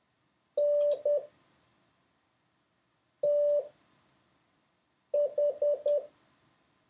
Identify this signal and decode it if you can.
Morse code, heard as a keyed tone, about 10 wpm, spaced as for 4 wpm, 580 Hz. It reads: N TH
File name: 6of6-test